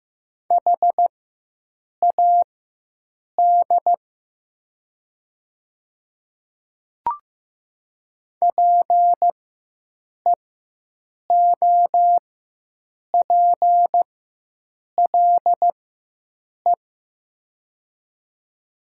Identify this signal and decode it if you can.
Morse code, heard as a keyed tone, about 15 wpm, spaced as for 7 wpm, 700 Hz. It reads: HAD PEOPLE
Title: 4x Character Spacing - Top 100 Words - 15wpm